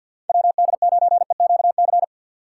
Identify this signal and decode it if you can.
Morse code, heard as a keyed tone, about 53 wpm, 700 Hz. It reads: WD9EQC
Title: USA Call Signs - 6 Characters - 4x Character Spacing - 50wpm